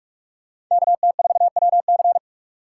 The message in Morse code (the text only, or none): KT4WC